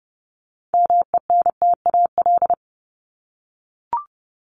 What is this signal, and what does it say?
Morse code, heard as a keyed tone, about 30 wpm, 700 Hz. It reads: MENTAL
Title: Top 501-750 Words 30wpm